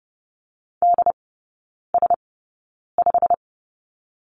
Morse code, heard as a keyed tone, about 30 wpm, spaced as for 9 wpm, 700 Hz. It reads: DS5
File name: course / 30:10wpm / 087 Lesson 15e - Number 5 - Three Characters 30wpm